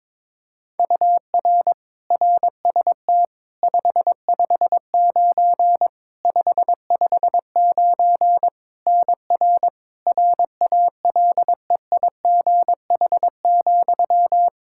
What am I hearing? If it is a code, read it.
Morse code, heard as a keyed tone, about 22 wpm, 700 Hz. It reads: UR RST 559 559 NR RALEIGH,
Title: K1USN SST QSO Practice 22wpm